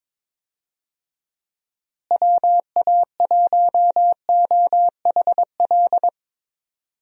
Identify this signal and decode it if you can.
Morse code, heard as a keyed tone, about 21 wpm, 700 Hz. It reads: WA1OHL
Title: USA Call Signs - 6 Characters - Repeated 2x - 22wpm